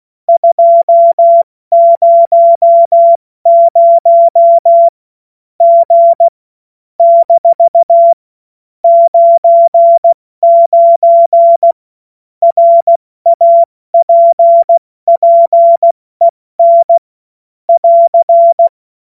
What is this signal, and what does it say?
Morse code, heard as a keyed tone, about 16 wpm, 680 Hz. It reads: 200 G - 99 RAPPEN <AR>
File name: Morsekurs / Witze / Witz 31. August 2021, 70 F80